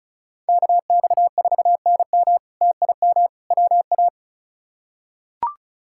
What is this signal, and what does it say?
Morse code, heard as a keyed tone, about 36 wpm, 700 Hz. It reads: KX4NM TIM WA
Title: K1USN SST QSO Practice - 35wpm